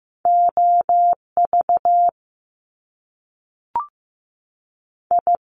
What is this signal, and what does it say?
Morse code, heard as a keyed tone, about 15 wpm, 700 Hz. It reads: OV I
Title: Top Two Letter Combinations in Words 15wpm